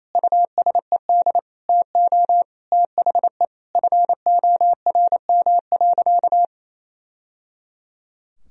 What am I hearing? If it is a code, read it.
Morse code, heard as a keyed tone, about 28 wpm, 700 Hz. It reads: USED TO THE FORM.
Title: Sentences from Top 1000 Words - Review Part 1 - 28wpm